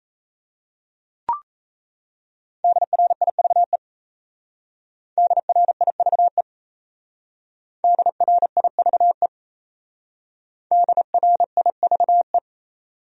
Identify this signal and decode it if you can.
Morse code, heard as a keyed tone, about 35 wpm, 700 Hz. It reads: DRIVE DRIVE DRIVE DRIVE
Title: Speed-Racing Top 500 Words 28wpm